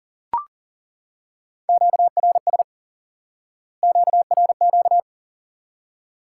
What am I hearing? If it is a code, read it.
Morse code, heard as a keyed tone, about 40 wpm, 700 Hz. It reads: QRS QRQ
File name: IQE Course 3 - Lesson 89 - YAGI - Two Elements - 40wpm